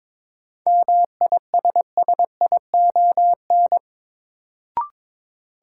Morse code, heard as a keyed tone, about 22 wpm, 700 Hz. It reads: MISSION
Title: Sentences from Top 1600 Words - New Word Drill Alt - 22wpm